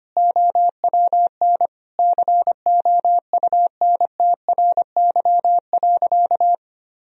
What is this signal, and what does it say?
Morse code, heard as a keyed tone, about 25 wpm, 700 Hz. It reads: OWN COUNTRY.
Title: Sentences from Top 1000 Words - Review Part 1 - 25wpm